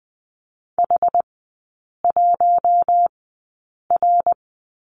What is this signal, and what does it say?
Morse code, heard as a keyed tone, about 20 wpm, spaced as for 8 wpm, 690 Hz. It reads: H1R
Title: 133 Lesson 22e - Question Mark - Three Characters 20wpm